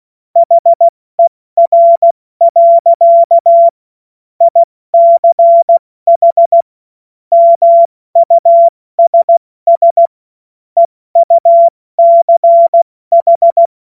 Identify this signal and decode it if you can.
Morse code, heard as a keyed tone, about 16 wpm, 680 Hz. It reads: HER. ICH MUSS EUCH
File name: Morsekurs / Witze / Witz 31. Januar 2023 70 F80